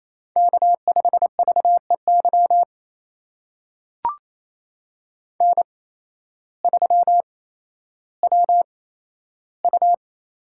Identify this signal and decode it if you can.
Morse code, heard as a keyed tone, about 28 wpm, spaced as for 13 wpm, 700 Hz. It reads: K5VEY N3WU
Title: USA Call Signs - 5 Characters - 8x Character Spacing - 28wpm